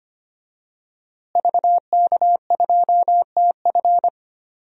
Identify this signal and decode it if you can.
Morse code, heard as a keyed tone, about 25 wpm, 700 Hz. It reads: VK2TF
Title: DXCC-100 Call Signs - 5 Characters - Repeated 3x - 25wpm